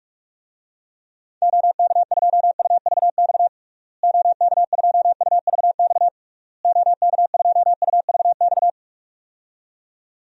Morse code, heard as a keyed tone, about 42 wpm, 700 Hz. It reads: OK2UVX OK2UVX OK2UVX
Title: DXCC-100 Call Signs - 6 Characters - Repeated 3x - 45wpm